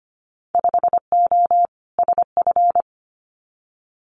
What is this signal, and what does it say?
Morse code, heard as a keyed tone, about 25 wpm, 700 Hz. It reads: 5O SF
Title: Winter Field Day Exchanges Encore 25wpm